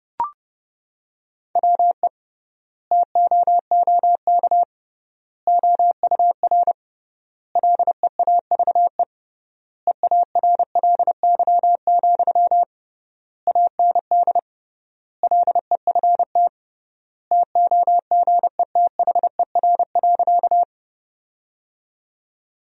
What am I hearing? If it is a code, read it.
Morse code, heard as a keyed tone, about 29 wpm, 700 Hz. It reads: WE TOOK OUR LEAVE EARLY, AND LEFT TOGETHER.
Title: 3x Word-Spacing - Sentences from Top 400 Words - 30wpm